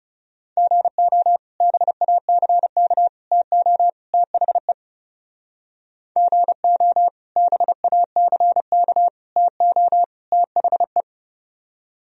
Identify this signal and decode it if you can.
Morse code, heard as a keyed tone, about 39 wpm, 700 Hz. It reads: GO BACK TO THE GO BACK TO THE